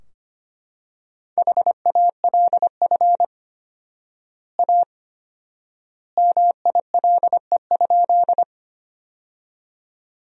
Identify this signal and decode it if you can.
Morse code, heard as a keyed tone, about 25 wpm, 700 Hz. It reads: HALF A MILE?
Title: 4x Word-Spacing - Sentences from Top 400 Words - 25wpm